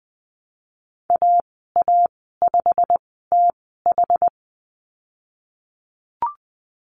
Morse code, heard as a keyed tone, about 20 wpm, spaced as for 14 wpm, 700 Hz. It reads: AA5TH
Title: 088 Lesson 15g - Number 5 - Call Signs 20wpm